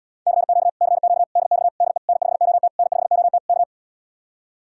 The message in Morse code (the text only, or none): CQ CQ DX DE N5OT N5OT K